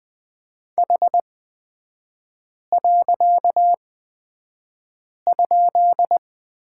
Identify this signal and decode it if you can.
Morse code, heard as a keyed tone, about 20 wpm, spaced as for 5 wpm, 700 Hz. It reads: H.?
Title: 255 Lesson 41e - Period - Three Characters 20wpm